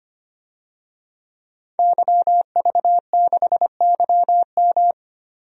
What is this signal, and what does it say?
Morse code, heard as a keyed tone, about 25 wpm, 700 Hz. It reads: YV6YM
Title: DXCC-100 Call Signs - 5 Characters - 25wpm